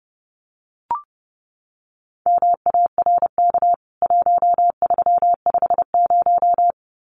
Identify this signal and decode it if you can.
Morse code, heard as a keyed tone, about 30 wpm, 700 Hz. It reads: MARK 1350
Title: CWT Exchange 30wpm